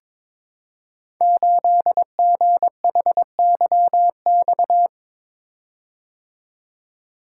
Morse code, heard as a keyed tone, about 22 wpm, 700 Hz. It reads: 8GHYX
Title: Code Groups - 5 Letter:Numbers - 22wpm